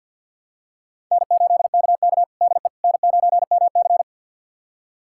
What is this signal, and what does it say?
Morse code, heard as a keyed tone, about 52 wpm, 700 Hz. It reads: N8KK DE N9MC
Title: USA Callsign DE Callsign - 4 Characters - 50wpm